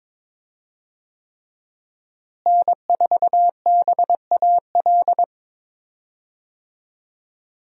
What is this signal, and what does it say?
Morse code, heard as a keyed tone, about 21 wpm, 700 Hz. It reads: N4BAL
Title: USA Call Signs - 5 Characters - Repeated 3x - 22wpm